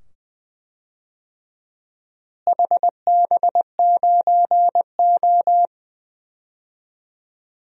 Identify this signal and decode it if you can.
Morse code, heard as a keyed tone, about 20 wpm, 700 Hz. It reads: HB9O
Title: DXCC-100 Call Signs - 4 Characters - Repeated 2x - 20wpm